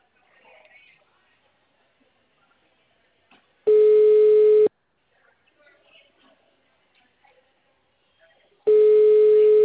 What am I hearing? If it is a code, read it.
Ringback tone (Europe, 425 Hz), with speech.